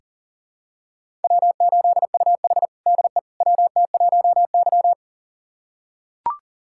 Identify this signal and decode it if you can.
Morse code, heard as a keyed tone, about 40 wpm, 700 Hz. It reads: W8UH DE WT1Y
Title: USA Callsign DE Callsign - 4 Characters - 40wpm